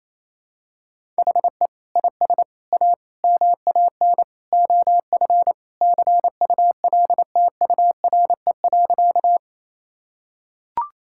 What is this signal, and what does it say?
Morse code, heard as a keyed tone, about 28 wpm, 700 Hz. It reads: HE IS A MAN OF CULTURE.
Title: Sentences from Top 1000 Words - Review Part 6 - 28wpm